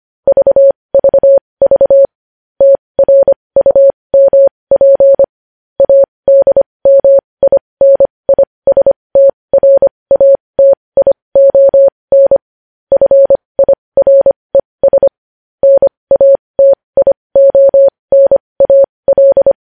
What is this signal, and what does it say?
Morse code, heard as a keyed tone, about 25 wpm, spaced as for 20 wpm, 550 Hz. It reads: VVV TRUMP ADMINISTRATION FIRES NATIONAL
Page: News headlines in morse code at 20 wpm